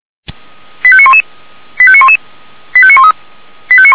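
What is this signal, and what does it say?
Five-tone selective call (ZVEI), heard as quick runs of five tones, tones of about 70 ms. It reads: 85911 85911 85913
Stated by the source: ZVEI five-tone selective call received with an SDR.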